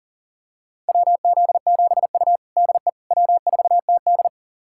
Morse code, heard as a keyed tone, about 40 wpm, 700 Hz. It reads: WZ7U DE W4TD
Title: USA Callsign DE Callsign - 4 Characters - 40wpm